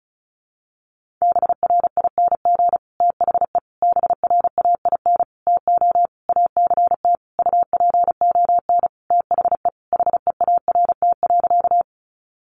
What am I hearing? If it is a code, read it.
Morse code, heard as a keyed tone, about 34 wpm, 700 Hz. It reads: BRING THE BRAIN TO ACT UPON THE HEART.
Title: Sentences from Top 1200 Words 35wpm